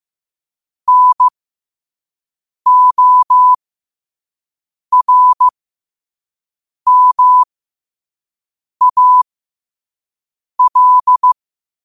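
Morse code, heard as a keyed tone, about 15 wpm, spaced as for 5 wpm, 1000 Hz. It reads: NORMAL
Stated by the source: Morse code practice at 5 wpm.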